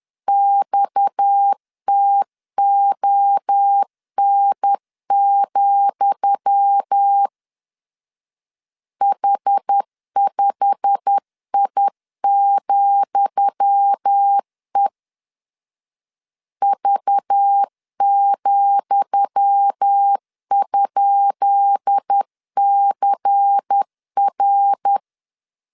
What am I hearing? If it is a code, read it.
Morse code, heard as a keyed tone, about 10 wpm, 800 Hz. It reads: XTON, H5I,E V,?CR
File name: LEZIONI / 10 SIGNS / 10D COMA 10wpm 3013